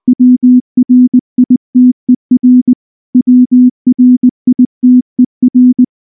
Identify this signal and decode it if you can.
Morse code, heard as a keyed tone, about 19 wpm, 250 Hz. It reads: WRITER WRITER
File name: morse-training-20WPM